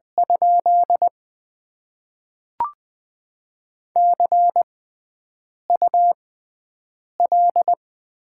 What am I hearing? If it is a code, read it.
Morse code, heard as a keyed tone, about 20 wpm, spaced as for 7 wpm, 700 Hz. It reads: ? CUL
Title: Top 90 QSO Elements - 6x Character Spacing - 20wpm